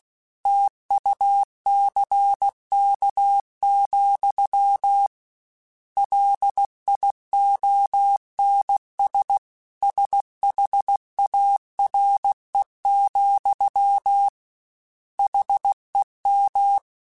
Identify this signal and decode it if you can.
Morse code, heard as a keyed tone, about 16 wpm, 800 Hz. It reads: TUCK, LIONS SHARE, HEM
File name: Misc / Misc-mp3s / Word Pairs @15